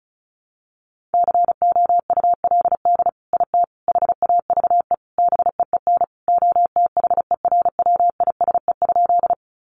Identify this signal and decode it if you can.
Morse code, heard as a keyed tone, about 34 wpm, 700 Hz. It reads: COULD IT HAVE BEEN OTHERWISE?
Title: Sentences from Top 1200 Words 35wpm